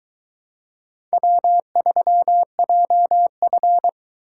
Morse code, heard as a keyed tone, about 20 wpm, 700 Hz. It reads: W3JF